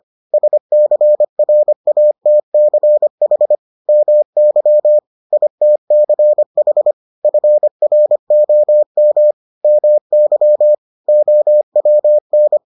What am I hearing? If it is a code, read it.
Morse code, heard as a keyed tone, about 25 wpm, 600 Hz. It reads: SCRATCH MY ITCH FROM MY OWN